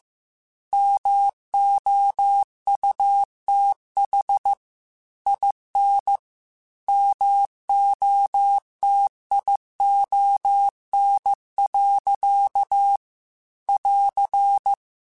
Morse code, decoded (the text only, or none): MOUTH IN MOTION. <AR>